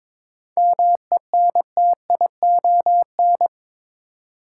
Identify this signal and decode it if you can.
Morse code, heard as a keyed tone, about 22 wpm, 700 Hz. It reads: MENTION
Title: Seven Letter Words - 22wpm